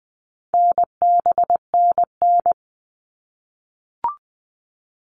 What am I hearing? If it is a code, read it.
Morse code, heard as a keyed tone, about 20 wpm, 700 Hz. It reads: NBNN
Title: All Cut Numbers - 4-Digit 20wpm